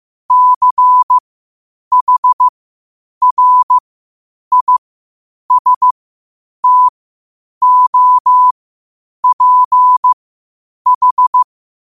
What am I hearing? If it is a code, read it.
Morse code, heard as a keyed tone, about 15 wpm, spaced as for 8 wpm, 1000 Hz. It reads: CHRISTOPH
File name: qotdfiles / 8wpm